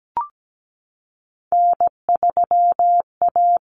Morse code, heard as a keyed tone, about 17 wpm, 700 Hz. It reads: N3A
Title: Common Cut Numbers - 3-Digit 17wpm